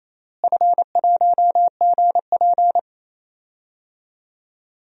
Morse code, heard as a keyed tone, about 28 wpm, 700 Hz. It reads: F1GP
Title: DXCC-100 Call Signs - 4 Characters - Repeated 3x - 28wpm